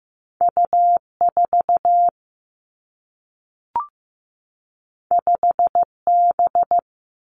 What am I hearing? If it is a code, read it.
Morse code, heard as a keyed tone, about 15 wpm, 700 Hz. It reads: U 4 5 B
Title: Two Challenging Letter-Number Combinations 15wpm